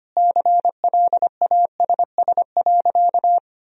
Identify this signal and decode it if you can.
Morse code, heard as a keyed tone, about 25 wpm, 700 Hz. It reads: CLASS.
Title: Sentences from Top 1000 Words - Review Part 2 - 25wpm